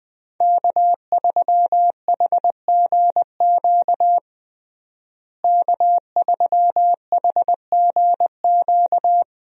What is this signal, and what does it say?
Morse code, heard as a keyed tone, about 20 wpm, 700 Hz. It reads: K3HGQ K3HGQ
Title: USA Call Signs - 5 Characters - Repeated 3x - 20wpm